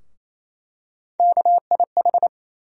Morse code, heard as a keyed tone, about 28 wpm, 700 Hz. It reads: KIH
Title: USA Call Sign Postfixes - 3 Letters 28wpm